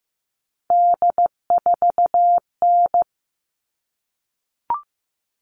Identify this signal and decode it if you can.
Morse code, heard as a keyed tone, about 15 wpm, 700 Hz. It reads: D4N
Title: All Cut Numbers - 3-Digit 15wpm